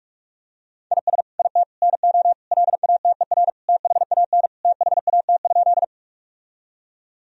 Morse code, heard as a keyed tone, about 47 wpm, 700 Hz. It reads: IS IT NO LATER THAN THAT?
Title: Sentences from Top 1000 Words - Review Part 4 - 45wpm